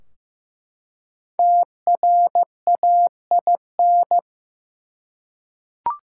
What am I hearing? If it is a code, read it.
Morse code, heard as a keyed tone, about 15 wpm, 700 Hz. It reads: TRAIN
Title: Five Letter Words - 15wpm